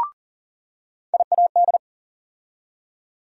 Morse code, heard as a keyed tone, about 40 wpm, 700 Hz. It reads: IAD
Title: USA Call Sign Postfixes - 3 Letters 40wpm